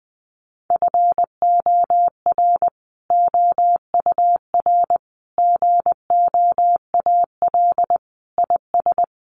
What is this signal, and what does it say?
Morse code, heard as a keyed tone, about 20 wpm, 700 Hz. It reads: FOR OUR GOAL IS
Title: Sentences from Top 1400 Words 20wpm